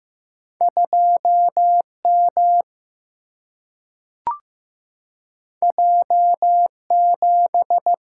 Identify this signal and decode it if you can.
Morse code, heard as a keyed tone, about 15 wpm, 700 Hz. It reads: 2 M J 7
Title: Single-Letter Single-Number 15wpm